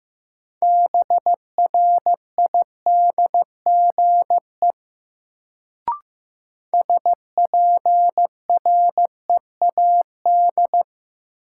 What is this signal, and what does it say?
Morse code, heard as a keyed tone, about 15 wpm, 700 Hz. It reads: BRIDGE SPREAD
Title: Sentences from Top 1200 Words - New Word Drill Alt - 15wpm